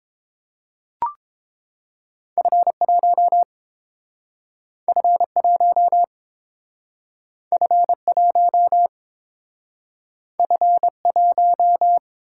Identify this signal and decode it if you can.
Morse code, heard as a keyed tone, about 30 wpm, 700 Hz. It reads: F1 F1 F1 F1